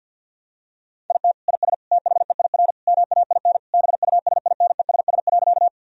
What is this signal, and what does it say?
Morse code, heard as a keyed tone, about 50 wpm, 700 Hz. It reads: IT IS THEIR MAIN BUSINESS.
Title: Sentences from Top 1000 Words - Review Part 2 - 50wpm